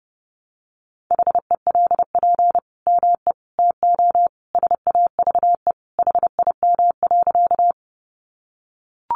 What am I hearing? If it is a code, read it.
Morse code, heard as a keyed tone, about 30 wpm, 700 Hz. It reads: HELP ME TO SAVE HIM.
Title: Sentences from Top 1100 Words 30wpm